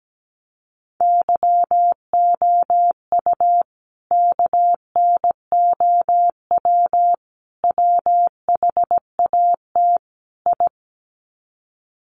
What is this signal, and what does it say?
Morse code, heard as a keyed tone, about 17 wpm, 700 Hz. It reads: YOU KNOW WHAT I
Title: Sets of 4 Words 17wpm